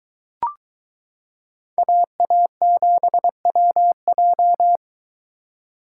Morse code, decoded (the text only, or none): AA7WJ